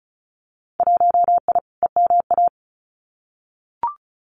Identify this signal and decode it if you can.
Morse code, heard as a keyed tone, about 34 wpm, 700 Hz. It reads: 1I EMA
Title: Winter Field Day Exchanges Encore 35wpm